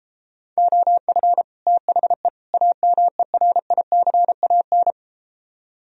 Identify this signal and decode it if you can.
Morse code, heard as a keyed tone, about 33 wpm, 700 Hz. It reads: OF THE AMERICAN